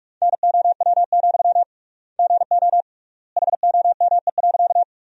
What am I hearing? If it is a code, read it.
Morse code, heard as a keyed tone, about 42 wpm, 700 Hz. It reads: NOW, GO HOME.
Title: 3x Word-Spacing - Sentences from Top 200 Words - 45wpm